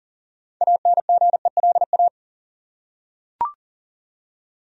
Morse code, heard as a keyed tone, about 40 wpm, 700 Hz. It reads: ANGELA
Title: USA Common Names - 40wpm